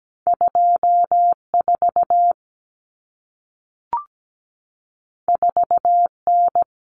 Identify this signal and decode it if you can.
Morse code, heard as a keyed tone, about 17 wpm, 700 Hz. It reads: 24 4N
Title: Common Cut Numbers - 2-Digit 17wpm